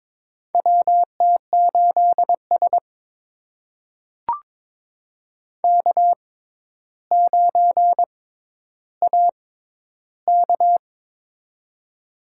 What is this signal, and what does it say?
Morse code, heard as a keyed tone, about 22 wpm, spaced as for 11 wpm, 700 Hz. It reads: WT8S K9AK
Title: USA Call Signs - 4 Characters - 6x Character Spacing - 22wpm